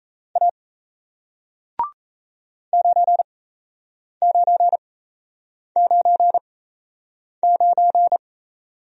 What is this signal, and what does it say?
Morse code, heard as a keyed tone, about 37 wpm, spaced as for 8 wpm, 700 Hz. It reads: A 9999